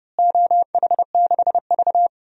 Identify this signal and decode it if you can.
Morse code, heard as a keyed tone, about 30 wpm, 700 Hz. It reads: OH6V
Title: DXCC-100 Call Signs - 4 Characters - Repeated 2x - 30wpm